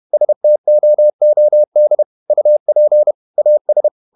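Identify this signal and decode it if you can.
Morse code, heard as a keyed tone, about 31 wpm, 600 Hz. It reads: STOOD UP AS
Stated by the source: Morse code at 30 wpm, the novel Never by Ken Follett in English, from CW med Gnister.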